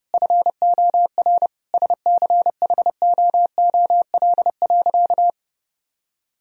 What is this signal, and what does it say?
Morse code, heard as a keyed tone, about 30 wpm, 700 Hz. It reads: FOR SCHOOL.
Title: Sentences from Top 1000 Words - Review Part 1 - 30wpm